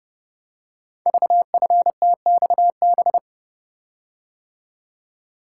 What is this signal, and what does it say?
Morse code, heard as a keyed tone, about 30 wpm, 700 Hz. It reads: VFTXB
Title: Code Groups - 5 Letters - 30wpm